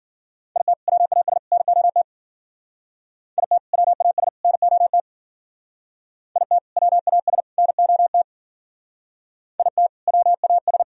Morse code, heard as a keyed tone, about 52 wpm, 700 Hz. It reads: IT WAS NOT IT WAS NOT IT WAS NOT IT WAS